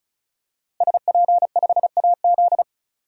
Morse code, heard as a keyed tone, about 34 wpm, 700 Hz. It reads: SP5AZ
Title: DXCC-100 Call Signs - 5 Characters - Repeated 3x - 35wpm